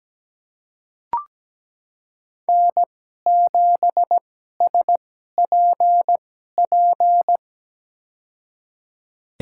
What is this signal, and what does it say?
Morse code, heard as a keyed tone, about 17 wpm, spaced as for 12 wpm, 700 Hz. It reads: N7SPP
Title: USA Call Signs - 2x Character Spacing - 17wpm